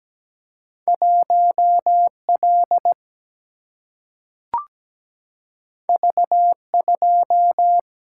Morse code, heard as a keyed tone, about 17 wpm, 700 Hz. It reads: 1L V2
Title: Single-Letter Single-Number 17wpm